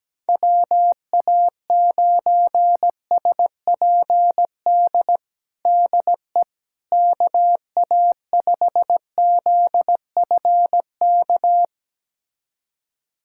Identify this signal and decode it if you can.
Morse code, heard as a keyed tone, about 17 wpm, 700 Hz. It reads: WA9SPD DE KA5ZFK
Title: USA Callsign DE Callsign - 6 Characters - 17wpm